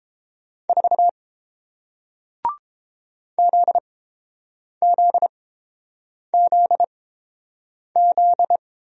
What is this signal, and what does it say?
Morse code, heard as a keyed tone, about 31 wpm, spaced as for 7 wpm, 700 Hz. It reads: 4 ZZZZ